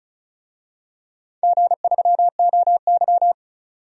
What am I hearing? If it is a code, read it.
Morse code, heard as a keyed tone, about 36 wpm, 700 Hz. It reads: G3OY
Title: DXCC-100 Call Signs - 4 Characters - 35wpm